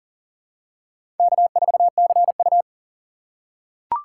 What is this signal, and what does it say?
Morse code, heard as a keyed tone, about 37 wpm, 700 Hz. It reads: K4CU
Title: USA Call Signs - 4 Characters - 3x Character Spacing - 40wpm